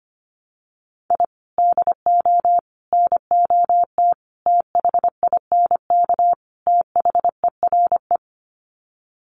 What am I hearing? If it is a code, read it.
Morse code, heard as a keyed tone, about 25 wpm, 700 Hz. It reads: I DO NOT THINK THERE
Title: Sets of 5 Words 25wpm